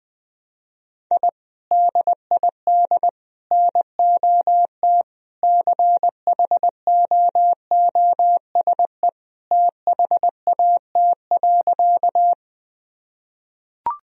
Morse code, heard as a keyed tone, about 20 wpm, 700 Hz. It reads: I DID NOT CHOOSE THAT.
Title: Sentences from Top 1000 Words - Review Part 4 - 20wpm